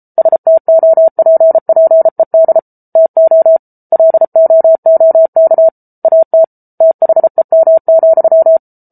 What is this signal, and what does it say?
Morse code, heard as a keyed tone, about 34 wpm, 650 Hz. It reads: STOPPED TO LOOK AT THEM,